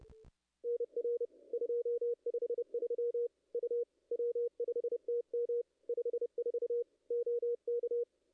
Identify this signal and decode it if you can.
Morse code, heard as a keyed tone, about 30 wpm, 440 Hz. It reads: NR 253 U W5TM 54 OK
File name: contests / ss-cw-k5tr-op-n5rz -0100